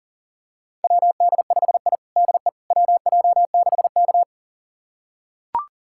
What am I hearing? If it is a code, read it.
Morse code, heard as a keyed tone, about 37 wpm, 700 Hz. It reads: WD5I DE WJ6K
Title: USA Callsign DE Callsign - 4 Characters - 40wpm